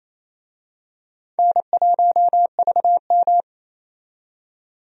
Morse code, heard as a keyed tone, about 28 wpm, 700 Hz. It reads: N1VM
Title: USA Call Signs - 4 Characters - 28wpm